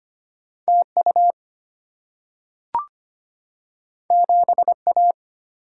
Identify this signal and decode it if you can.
Morse code, heard as a keyed tone, about 25 wpm, 700 Hz. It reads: TU 7A
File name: Two Letter-Number Combinations - 25wpm